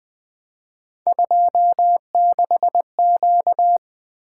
Morse code, heard as a keyed tone, about 20 wpm, 700 Hz. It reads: 26Q
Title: Three Letter-Number Combinations - 20wpm